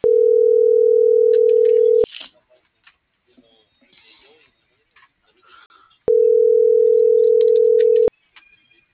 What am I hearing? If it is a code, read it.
Ringback tone (North America), 440 + 480 Hz.